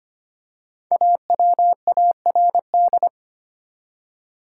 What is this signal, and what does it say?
Morse code, heard as a keyed tone, about 25 wpm, 700 Hz. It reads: AWARD
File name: 5 Letter Words - First 3 Letters Given - 25wpm